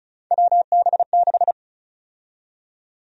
Morse code, heard as a keyed tone, about 34 wpm, 700 Hz. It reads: WB6
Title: USA Call Sign Prefixes - Repeated 2x - 35wpm